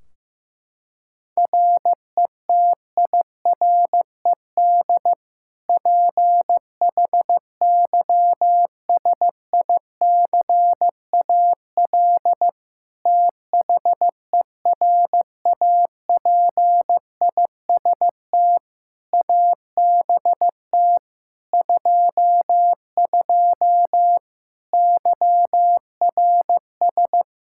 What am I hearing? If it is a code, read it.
Morse code, heard as a keyed tone, about 15 wpm, 700 Hz. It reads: RETIRED PHYSICAL THERAPIST ABT 22 YRS
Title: Practice QSO - Exchange 3 - More Info - 15wpm